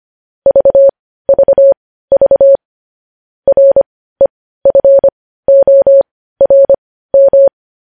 Morse code, heard as a keyed tone, about 25 wpm, spaced as for 15 wpm, 550 Hz. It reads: VVV REFORM
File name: podcast / News headlines-1x-25x15